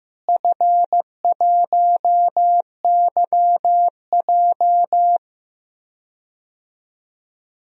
Morse code, heard as a keyed tone, about 15 wpm, 700 Hz. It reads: F1YJ
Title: DXCC-100 Call Signs - 4 Characters - Repeated 2x - 15wpm